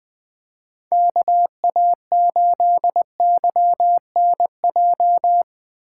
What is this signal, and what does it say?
Morse code, heard as a keyed tone, about 20 wpm, 700 Hz. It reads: KA8YNJ
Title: USA Call Signs - 6 Characters - 2x Character Spacing - 20wpm